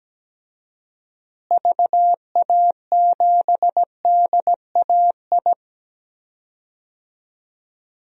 Morse code, heard as a keyed tone, about 17 wpm, 700 Hz. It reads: VA7DAI
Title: DXCC-100 Call Signs - 6 Characters - Repeated 3x - 17wpm